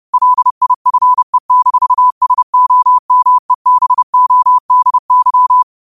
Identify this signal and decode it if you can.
Morse code, heard as a keyed tone, about 30 wpm, 1000 Hz. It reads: LIFE-SOMEBODY